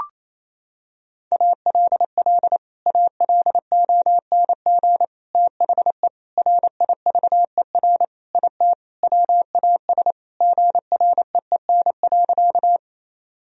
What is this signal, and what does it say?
Morse code, heard as a keyed tone, about 27 wpm, 700 Hz. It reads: ALL ALONG THE RIVER IT WAS GREEN.
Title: Sentences from Top 1000 Words - Review Part 2 - 28wpm